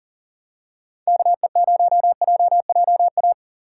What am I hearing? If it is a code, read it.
Morse code, heard as a keyed tone, about 40 wpm, 700 Hz. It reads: KE0JJA
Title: USA Call Signs - 3x Character Spacing - 40wpm